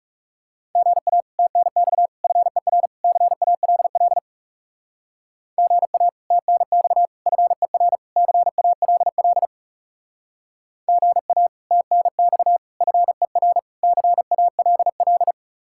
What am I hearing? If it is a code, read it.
Morse code, heard as a keyed tone, about 37 wpm, 700 Hz. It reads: GA TNX FER CALL GA TNX FER CALL GA TNX FER CALL
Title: Speed-Racing Common QSO Phrases - 30wpm